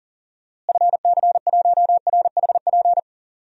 Morse code, heard as a keyed tone, about 40 wpm, 700 Hz. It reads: FC1RHP
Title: DXCC-100 Call Signs - 6 Characters - Repeated 3x - 40wpm